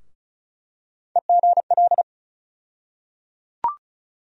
Morse code, heard as a keyed tone, about 34 wpm, 700 Hz. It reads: EGL
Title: USA Call Sign Postfixes - 3 Letters 35wpm